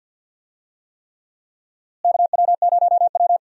Morse code, heard as a keyed tone, about 52 wpm, 700 Hz. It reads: KW0W
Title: USA Call Signs - 4 Characters - Repeated 3x - 50wpm